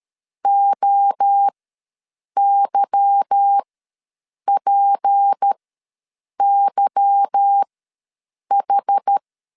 Morse code, heard as a keyed tone, about 13 wpm, spaced as for 7 wpm, 800 Hz. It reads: OYPYH
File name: LEZIONI / 8 YJ45 / 8 Y ws 12wpm 3409